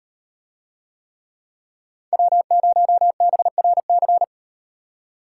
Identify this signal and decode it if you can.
Morse code, heard as a keyed tone, about 35 wpm, 700 Hz. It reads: W0BRC